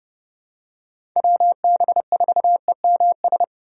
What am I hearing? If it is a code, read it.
Morse code, heard as a keyed tone, about 31 wpm, 700 Hz. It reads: WB4EMS
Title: USA Call Signs - 6 Characters - 30wpm